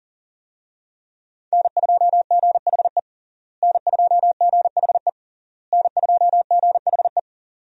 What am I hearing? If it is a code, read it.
Morse code, heard as a keyed tone, about 40 wpm, 700 Hz. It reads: N2GHE N2GHE N2GHE
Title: USA Call Signs - 5 Characters - Repeated 3x - 40wpm